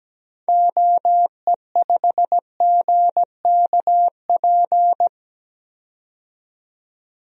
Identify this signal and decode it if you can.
Morse code, heard as a keyed tone, about 17 wpm, 700 Hz. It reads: OE5GKP
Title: DXCC-100 Call Signs - 6 Characters - 17wpm